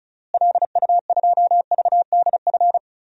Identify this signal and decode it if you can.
Morse code, heard as a keyed tone, about 36 wpm, 700 Hz. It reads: LU2VDF
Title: DXCC-100 Call Signs - 6 Characters - Repeated 3x - 35wpm